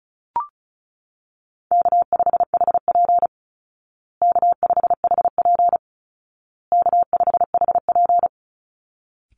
Morse code, heard as a keyed tone, about 34 wpm, 700 Hz. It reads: K5HP K5HP K5HP
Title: Call Signs Repeated 2x - 35wpm